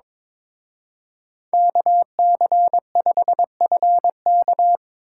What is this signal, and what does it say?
Morse code, heard as a keyed tone, about 25 wpm, 700 Hz. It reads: KC5FK
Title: Speed Racing - USA 2x2 Call Signs 22wpm